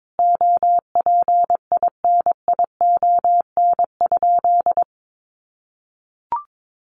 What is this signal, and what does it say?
Morse code, heard as a keyed tone, about 22 wpm, 700 Hz. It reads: OPINION?
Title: Sentences from Top 1200 Words 22wpm